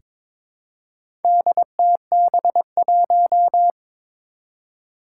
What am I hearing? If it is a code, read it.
Morse code, heard as a keyed tone, about 22 wpm, 700 Hz. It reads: DTB1
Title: Four Challenging Letter-Number Combinations 22wpm